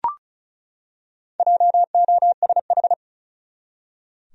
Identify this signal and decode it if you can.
Morse code, heard as a keyed tone, about 36 wpm, 700 Hz. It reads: JOSH
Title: USA Common Names - Max 5 Letters - 35wpm